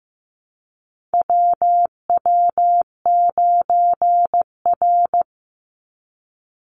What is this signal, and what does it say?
Morse code, heard as a keyed tone, about 15 wpm, 700 Hz. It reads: WW9R
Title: Call Signs Repeated 3x - 15wpm